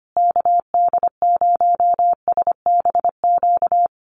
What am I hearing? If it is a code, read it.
Morse code, heard as a keyed tone, about 25 wpm, 700 Hz. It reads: KD0SBQ
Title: USA Call Signs Repeated 3x 25wpm